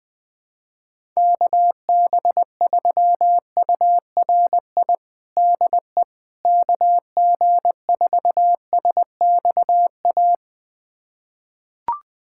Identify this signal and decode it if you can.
Morse code, heard as a keyed tone, about 21 wpm, 700 Hz. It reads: KB3URI DE KG4SXA
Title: USA Callsign DE Callsign 20wpm